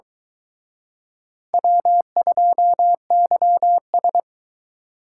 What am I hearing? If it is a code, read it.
Morse code, heard as a keyed tone, about 22 wpm, 700 Hz. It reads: W2YS